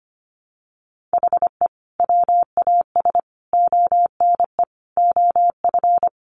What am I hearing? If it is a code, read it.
Morse code, heard as a keyed tone, about 25 wpm, 700 Hz. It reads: HE WAS ONE OF
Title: Sets of 5 Words 25wpm